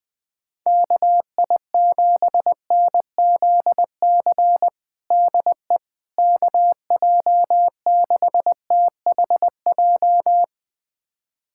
Morse code, heard as a keyed tone, about 20 wpm, 700 Hz. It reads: KI7NZC DE KJ6THJ
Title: USA Callsign DE Callsign - 6 Characters - 20wpm